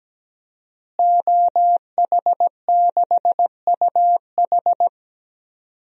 Morse code, heard as a keyed tone, about 17 wpm, 700 Hz. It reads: OH6UH
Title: DXCC-100 Call Signs - 5 Characters - Repeated 2x - 17wpm